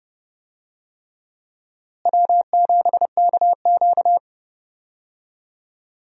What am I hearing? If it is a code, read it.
Morse code, heard as a keyed tone, about 30 wpm, 700 Hz. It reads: W7KQ
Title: USA Call Signs - 4 Characters - Repeated 3x - 30wpm